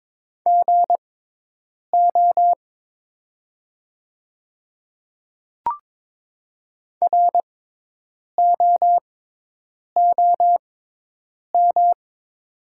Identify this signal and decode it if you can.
Morse code, heard as a keyed tone, about 22 wpm, spaced as for 8 wpm, 700 Hz. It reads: GO ROOM
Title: 6x Character Spacing - Top 300 Words - 22wpm